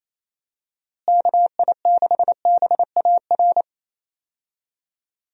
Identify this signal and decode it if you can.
Morse code, heard as a keyed tone, about 28 wpm, 700 Hz. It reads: KI6BAR
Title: USA Call Signs - 6 Characters - Repeated 2x - 28wpm